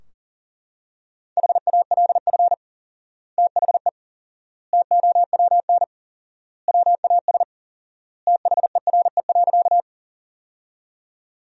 Morse code, heard as a keyed tone, about 37 wpm, 700 Hz. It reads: HALF THE TOWN WAS THERE.
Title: 4x Word-Spacing - Sentences from Top 500 Words - 40wpm